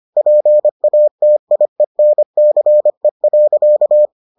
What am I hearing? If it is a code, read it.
Morse code, heard as a keyed tone, about 25 wpm, 600 Hz. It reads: PATIENCE.